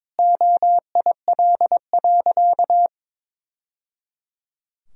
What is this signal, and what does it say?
Morse code, heard as a keyed tone, about 22 wpm, 700 Hz. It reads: OIL.
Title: Sentences from Top 1000 Words - Review Part 4 - 22wpm